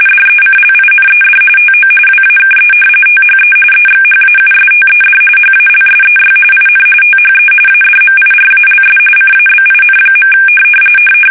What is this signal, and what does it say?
RTTY, about 45.45 baud. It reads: DECLINED TO WAKE THE EXHAUSTED HALSEY IN THE ADMIRALS CABIN.
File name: fsk850 / battleshipnj